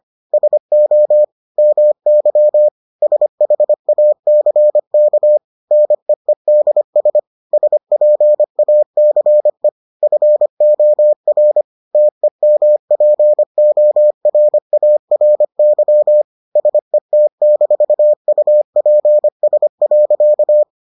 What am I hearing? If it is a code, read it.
Morse code, heard as a keyed tone, about 25 wpm, 600 Hz. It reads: SO MY SHACK NEEDS SPACE FOR TEMPORARY SET-UPS.